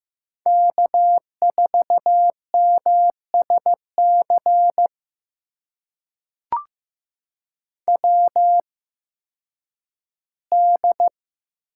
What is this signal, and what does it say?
Morse code, heard as a keyed tone, about 15 wpm, spaced as for 10 wpm, 700 Hz. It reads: K4MSC WD
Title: USA Call Signs - 8x Character Spacing - 15wpm